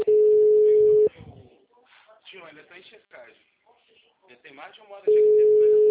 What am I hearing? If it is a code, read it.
Ringback tone (Europe, 425 Hz), with speech.